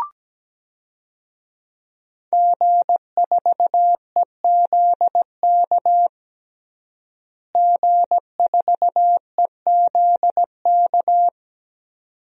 Morse code, heard as a keyed tone, about 17 wpm, 700 Hz. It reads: G4EZK G4EZK
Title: DXCC-100 Call Signs - 5 Characters - Repeated 2x - 17wpm